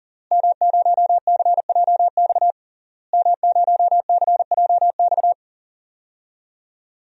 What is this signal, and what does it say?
Morse code, heard as a keyed tone, about 40 wpm, 700 Hz. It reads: M0CJX M0CJX
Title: DXCC-100 Call Signs - 5 Characters - Repeated 3x - 40wpm